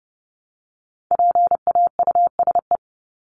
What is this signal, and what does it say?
Morse code, heard as a keyed tone, about 30 wpm, 700 Hz. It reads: PAUSE
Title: Top 1501-1750 Words 30wpm